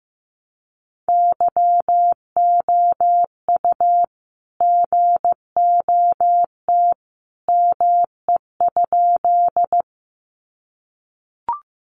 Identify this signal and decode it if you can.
Morse code, heard as a keyed tone, about 15 wpm, 700 Hz. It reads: YOU GOT ME?
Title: Sentences from Top 300 Words 15wpm